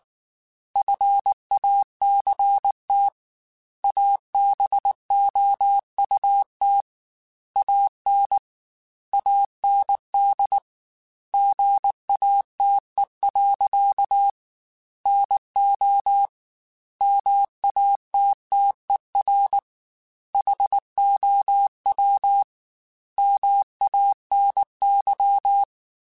Morse code, heard as a keyed tone, about 19 wpm, 800 Hz. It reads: FACT ABOUT AN AND GATE. NO MATTER HOW MANY
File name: Technical Subjects / Technical Subjects-mp3s / AND Gate @18